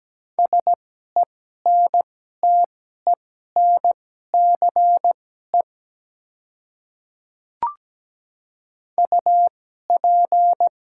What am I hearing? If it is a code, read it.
Morse code, heard as a keyed tone, about 17 wpm, spaced as for 12 wpm, 700 Hz. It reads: SENTENCE UP
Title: 2x Character Spacing - Top 200 Words - 17wpm